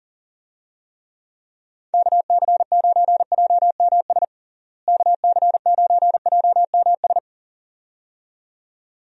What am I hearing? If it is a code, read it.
Morse code, heard as a keyed tone, about 40 wpm, 700 Hz. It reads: KC9JMS KC9JMS
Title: USA Call Signs - 6 Characters - Repeated 2x - 40wpm